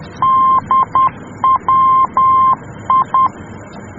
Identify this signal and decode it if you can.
Morse code, heard as a keyed tone, about 10 wpm, 1050 Hz. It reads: DWI
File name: germany / mx / DWI-D